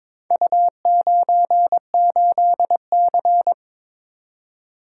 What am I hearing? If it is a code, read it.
Morse code, heard as a keyed tone, about 22 wpm, 700 Hz. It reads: U98C
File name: practice / 22wpm / Four Letter-Number Combinations - 22wpm